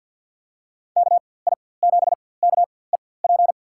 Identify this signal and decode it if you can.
Morse code, heard as a keyed tone, about 47 wpm, spaced as for 23 wpm, 700 Hz. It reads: KI7KEP
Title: USA Call Signs - 6 Characters - 4x Character Spacing - 50wpm